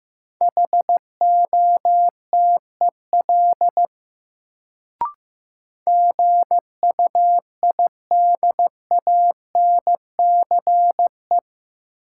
Morse code, heard as a keyed tone, about 15 wpm, 700 Hz. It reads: HOTEL GUIDANCE
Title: Sentences from Top 1600 Words - New Word Drill - 15wpm